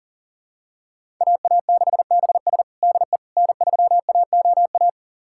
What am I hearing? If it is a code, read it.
Morse code, heard as a keyed tone, about 40 wpm, 700 Hz. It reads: AA6BS DE N3AOA
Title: USA Callsign DE Callsign - 5 Characters - 40wpm